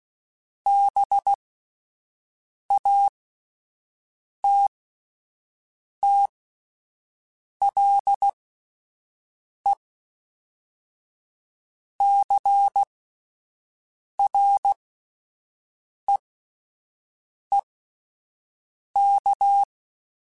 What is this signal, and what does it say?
Morse code, heard as a keyed tone, about 16 wpm, spaced as for 5 wpm, 800 Hz. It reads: BATTLE CREEK
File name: fromSMs / FromSMs mp3s / Cities@5